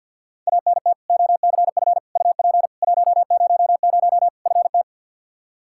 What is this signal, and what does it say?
Morse code, heard as a keyed tone, about 52 wpm, 700 Hz. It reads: ANT OCF UP 100 FT
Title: Practice QSO - Exchange 2 - Info - 50wpm